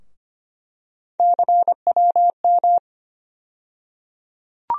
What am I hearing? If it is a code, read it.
Morse code, heard as a keyed tone, about 25 wpm, 700 Hz. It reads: CWM
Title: USA Call Sign Postfixes - 3 Letters 25wpm